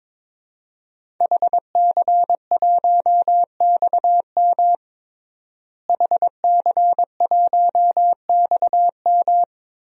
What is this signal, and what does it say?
Morse code, heard as a keyed tone, about 22 wpm, 700 Hz. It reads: HC1XM HC1XM
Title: DXCC-100 Call Signs - 5 Characters - Repeated 3x - 22wpm